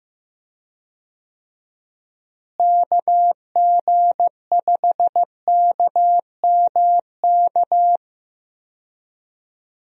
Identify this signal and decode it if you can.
Morse code, heard as a keyed tone, about 15 wpm, 700 Hz. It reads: KG5KMK
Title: USA Call Signs - 6 Characters - Repeated 2x - 15wpm